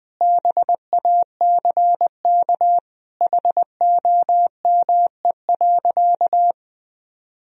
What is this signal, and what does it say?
Morse code, heard as a keyed tone, about 20 wpm, 700 Hz. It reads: BACK HOME.
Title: Sentences from Top 1000 Words - Review Part 2 - 20wpm